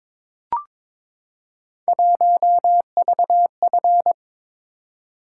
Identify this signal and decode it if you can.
Morse code, heard as a keyed tone, about 23 wpm, 700 Hz. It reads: 1VF
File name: practice / 22wpm / Three Letter-Number Combinations - 22wpm